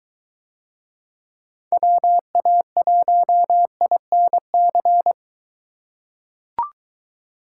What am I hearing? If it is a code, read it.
Morse code, heard as a keyed tone, about 22 wpm, 700 Hz. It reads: WA1INC